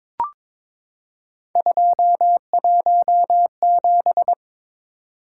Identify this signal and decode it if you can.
Morse code, heard as a keyed tone, about 22 wpm, 700 Hz. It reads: 217
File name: practice / 22wpm / Three Letter-Number Combinations - 22wpm